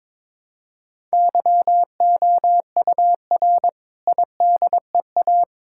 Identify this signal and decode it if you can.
Morse code, heard as a keyed tone, about 22 wpm, 700 Hz. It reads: YOUR IDEA
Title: Sentences from Top 1000 Words - Review Part 6 - 22wpm